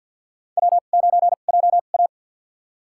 Morse code, heard as a keyed tone, about 49 wpm, spaced as for 35 wpm, 700 Hz. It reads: W9JA
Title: USA Call Signs - 4 Characters - 2x Character Spacing - 50wpm